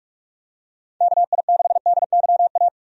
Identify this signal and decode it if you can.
Morse code, heard as a keyed tone, about 46 wpm, 700 Hz. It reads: KI6DYA
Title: USA Call Signs - 6 Characters - 45wpm